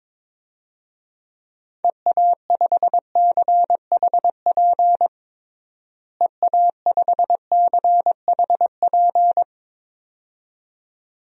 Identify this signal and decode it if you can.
Morse code, heard as a keyed tone, about 21 wpm, 700 Hz. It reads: EA5CHP EA5CHP
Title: DXCC-100 Call Signs - 6 Characters - Repeated 2x - 22wpm